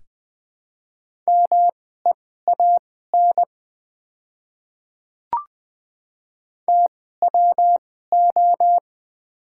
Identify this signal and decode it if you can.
Morse code, heard as a keyed tone, about 20 wpm, spaced as for 15 wpm, 700 Hz. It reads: MEAN TWO
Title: 2x Character Spacing - Top 200 Words - 20wpm